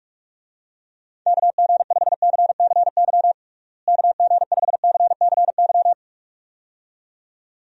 Morse code, heard as a keyed tone, about 47 wpm, 700 Hz. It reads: KG5CCY KG5CCY
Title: USA Call Signs - 6 Characters - Repeated 2x - 45wpm